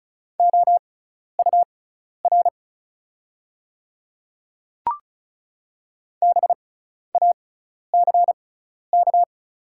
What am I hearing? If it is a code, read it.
Morse code, heard as a keyed tone, about 36 wpm, spaced as for 12 wpm, 700 Hz. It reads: OUR BACK
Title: 6x Character Spacing - Top 200 Words - 35wpm